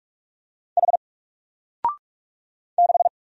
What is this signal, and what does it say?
Morse code, heard as a keyed tone, about 45 wpm, 700 Hz. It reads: H 6
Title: Single Letter-Number 45wpm